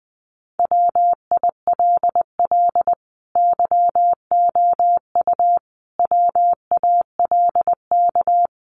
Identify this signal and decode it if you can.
Morse code, heard as a keyed tone, about 20 wpm, 700 Hz. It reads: WILL YOU WALK
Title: Sentences from Top 1100 Words 20wpm